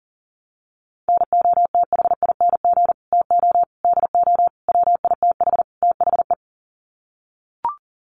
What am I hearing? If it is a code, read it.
Morse code, heard as a keyed tone, about 40 wpm, 700 Hz. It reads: NOTHING TO DO WITH THE
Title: Sets of 5 Words 40wpm